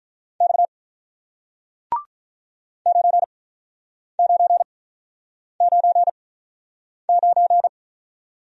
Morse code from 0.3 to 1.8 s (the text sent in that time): X